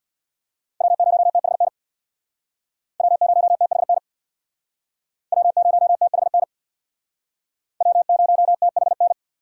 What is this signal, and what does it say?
Morse code, heard as a keyed tone, about 63 wpm, 700 Hz. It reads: W0THN W0THN W0THN W0THN